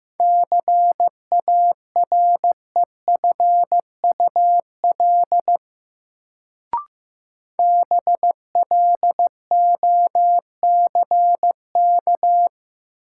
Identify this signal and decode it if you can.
Morse code, heard as a keyed tone, about 15 wpm, 700 Hz. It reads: CAREFUL BLOCK
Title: Sentences from Top 1600 Words - New Word Drill Alt - 15wpm